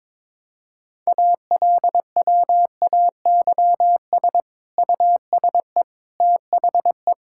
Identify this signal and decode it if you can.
Morse code, heard as a keyed tone, about 22 wpm, 700 Hz. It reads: ALWAYS USE THE
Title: Sentences from Top 1000 Words - Review Part 6 - 22wpm